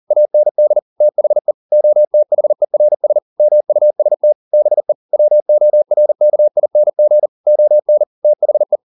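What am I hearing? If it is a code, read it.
Morse code, heard as a keyed tone, about 40 wpm, 600 Hz. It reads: AND THE OTHERS MUST BE WORKING ON THE
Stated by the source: Morse code at 40 wpm, practice text, CW med Gnister.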